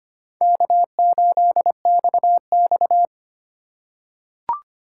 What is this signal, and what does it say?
Morse code, heard as a keyed tone, about 25 wpm, 700 Hz. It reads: K8XX
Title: USA Call Signs - 4 Characters - 3x Character Spacing - 25wpm